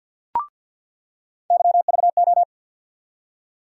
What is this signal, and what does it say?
Morse code, heard as a keyed tone, about 49 wpm, 700 Hz. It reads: YVO
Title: USA Call Sign Postfixes - 3 Letters 50wpm